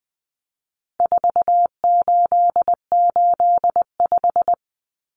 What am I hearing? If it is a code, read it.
Morse code, heard as a keyed tone, about 20 wpm, 700 Hz. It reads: 4885
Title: Common Cut Numbers - 4-Digit 20wpm